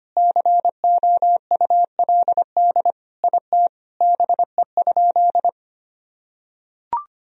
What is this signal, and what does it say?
Morse code, heard as a keyed tone, about 25 wpm, 700 Hz. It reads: COULD IT BE?
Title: Sentences from Top 1000 Words - Review Part 2 - 25wpm